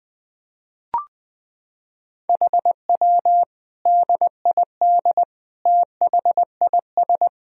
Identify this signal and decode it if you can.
Morse code, heard as a keyed tone, about 21 wpm, 700 Hz. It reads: HW DID THIS
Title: Sentences from Top 400 Words and Abbreviations 20wpm